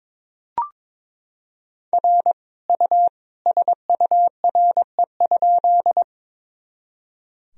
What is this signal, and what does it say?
Morse code, heard as a keyed tone, about 22 wpm, 700 Hz. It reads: R U SURE?
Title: Sentences from Top 300 Words and Abbreviations 22wpm